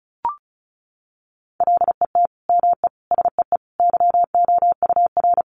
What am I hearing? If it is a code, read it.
Morse code, heard as a keyed tone, about 34 wpm, 700 Hz. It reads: LET ME SEE YOUR
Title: Sentences from Top 500 Words 35wpm